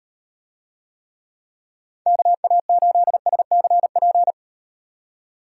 Morse code, heard as a keyed tone, about 34 wpm, 700 Hz. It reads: KA8SCP